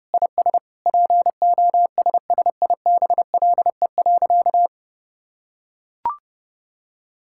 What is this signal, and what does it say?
Morse code, heard as a keyed tone, about 32 wpm, 700 Hz. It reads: IS POSSIBLE.
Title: Sentences from Top 1000 Words - Review Part 6 - 30wpm